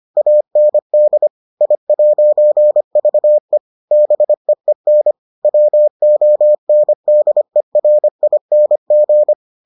Morse code, heard as a keyed tone, about 25 wpm, 600 Hz. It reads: AND I'VE BEEN WONDERING